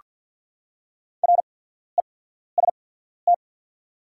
Morse code, heard as a keyed tone, about 48 wpm, spaced as for 14 wpm, 690 Hz. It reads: REST
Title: 8x Character Spacing - Top 500 Words - 50wpm